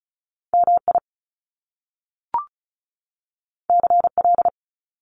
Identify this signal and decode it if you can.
Morse code, heard as a keyed tone, about 36 wpm, 700 Hz. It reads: MI CL
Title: Top Two Letter Combinations in Words 35wpm